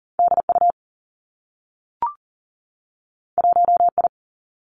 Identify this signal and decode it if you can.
Morse code, heard as a keyed tone, about 40 wpm, 700 Hz. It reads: D U 1 I
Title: Two Challenging Letter-Number Combinations 40wpm